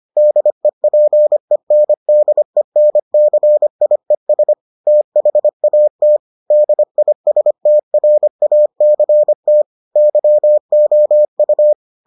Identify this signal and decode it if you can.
Morse code, heard as a keyed tone, about 25 wpm, 600 Hz. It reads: DEPENDENCIES THAT DISTRACT YOU